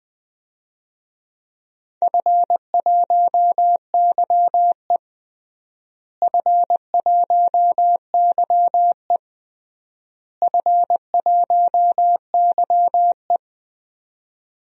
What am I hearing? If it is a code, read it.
Morse code, heard as a keyed tone, about 21 wpm, 700 Hz. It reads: F1YE F1YE F1YE
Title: DXCC-100 Call Signs - 4 Characters - Repeated 3x - 20wpm